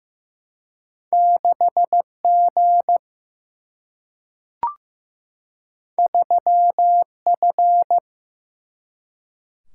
Morse code, heard as a keyed tone, about 15 wpm, 700 Hz. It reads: 6G 3F
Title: Single-Letter Single-Number 15wpm